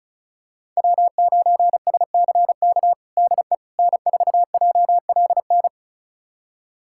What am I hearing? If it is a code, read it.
Morse code, heard as a keyed tone, about 36 wpm, 700 Hz. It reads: W9SCK DE N4JLN
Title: USA Callsign DE Callsign - 5 Characters - 35wpm